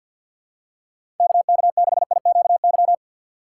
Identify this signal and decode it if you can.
Morse code, heard as a keyed tone, about 51 wpm, 700 Hz. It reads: KK6IQY
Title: USA Call Signs - 6 Characters - 4x Character Spacing - 50wpm